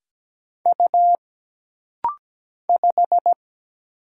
Morse code, heard as a keyed tone, about 17 wpm, 700 Hz. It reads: U 5
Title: Single Letter-Number Alt - Rapid-Fire 17wpm